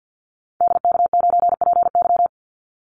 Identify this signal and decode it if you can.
Morse code, heard as a keyed tone, about 50 wpm, 700 Hz. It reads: DK9PY
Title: Call Signs - 50wpm